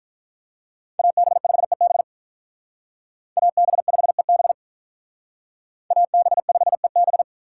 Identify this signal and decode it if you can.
Morse code, heard as a keyed tone, about 46 wpm, 700 Hz. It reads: AB5EB AB5EB AB5EB